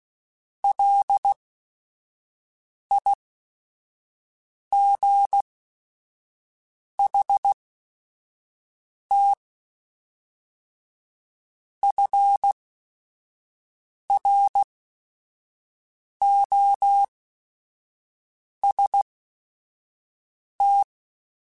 Morse code, heard as a keyed tone, about 16 wpm, spaced as for 5 wpm, 800 Hz. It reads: LIGHT FROST
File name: Weather Sets / Weather Sets-mp3s / Weather 4 28 @5